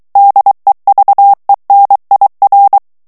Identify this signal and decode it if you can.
Morse code, heard as a keyed tone, about 23 wpm, 800 Hz. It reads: DEVENIR
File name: FRANCAIS TEXTE CLAIR 2 VITESSE 22 MOTS MINUTES